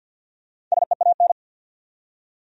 Morse code, heard as a keyed tone, about 47 wpm, 700 Hz. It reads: SEAN
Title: USA Common Names - 50wpm